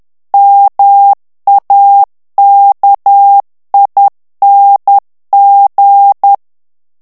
Morse code, heard as a keyed tone, about 10 wpm, 800 Hz. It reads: MAKING
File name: ANGLAIS texte clair 3 vitesse 10 mots minute